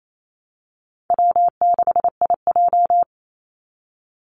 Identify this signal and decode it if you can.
Morse code, heard as a keyed tone, about 28 wpm, 700 Hz. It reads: W6IJ
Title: Easy Call Signs Repeated 2x - 28wpm